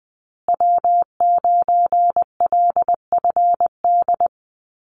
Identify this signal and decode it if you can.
Morse code, heard as a keyed tone, about 20 wpm, 700 Hz. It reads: W9LFD
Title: USA Call Signs Repeated 2x 20wpm